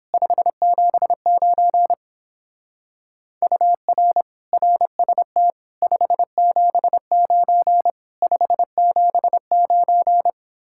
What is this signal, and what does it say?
Morse code, heard as a keyed tone, about 31 wpm, 700 Hz. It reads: 579 UR RST 579 579